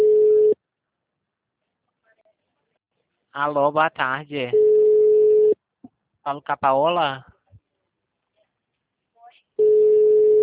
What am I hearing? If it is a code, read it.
Ringback tone (Europe, 425 Hz), with speech.
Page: Rio 3 - Ura